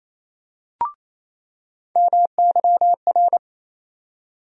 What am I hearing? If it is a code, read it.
Morse code, heard as a keyed tone, about 28 wpm, 700 Hz. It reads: MYR
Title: USA Call Sign Postfixes - 3 Letters 28wpm